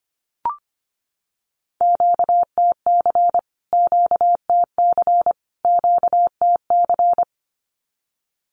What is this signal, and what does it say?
Morse code, heard as a keyed tone, about 25 wpm, 700 Hz. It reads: QTC QTC QTC
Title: Q-Codes Repeated 3x 25wpm